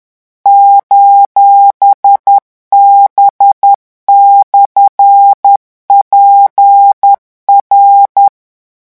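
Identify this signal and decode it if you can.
Morse code, heard as a keyed tone, about 10 wpm, 800 Hz. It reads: :B/PR
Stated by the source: French Morse code revision lesson 18 at 10 words per minute.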